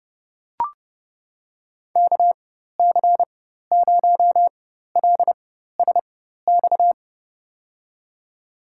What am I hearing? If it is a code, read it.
Morse code, heard as a keyed tone, about 30 wpm, spaced as for 14 wpm, 700 Hz. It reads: KC0LSX
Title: USA Call Signs - 4x Character Spacing - 30wpm